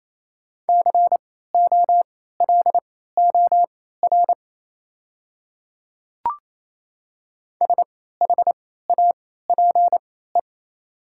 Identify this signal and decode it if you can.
Morse code, heard as a keyed tone, about 28 wpm, spaced as for 16 wpm, 700 Hz. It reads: COLOR SHAPE
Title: 3x Character Spacing - Top 500 Words - 28wpm